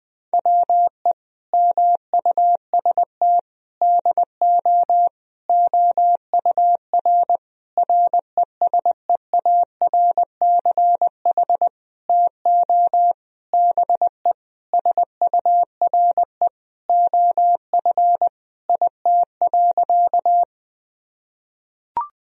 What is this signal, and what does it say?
Morse code, heard as a keyed tone, about 20 wpm, 700 Hz. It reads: WE MUST DO OUR RESEARCH TO BE SURE OF IT.
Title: Sentences from Top 1000 Words - Review Part 4 - 20wpm